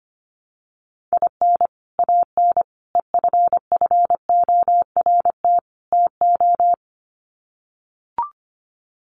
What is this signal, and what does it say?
Morse code, heard as a keyed tone, about 25 wpm, 700 Hz. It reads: IN AN EFFORT TO
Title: Sets of 4 Words 25wpm